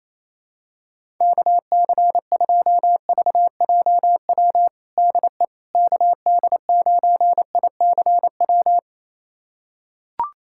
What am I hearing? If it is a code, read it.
Morse code, heard as a keyed tone, about 28 wpm, 700 Hz. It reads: KC2VJW DE KD9ICW
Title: USA Callsign DE Callsign 28wpm